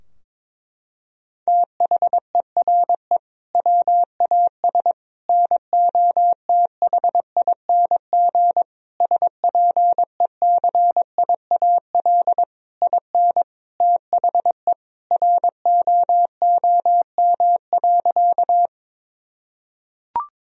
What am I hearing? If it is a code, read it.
Morse code, heard as a keyed tone, about 22 wpm, 700 Hz. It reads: THERE WAS NOTHING SPECIAL IN THE ROOM.
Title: Sentences from Top 1000 Words - Review Part 3 - 22wpm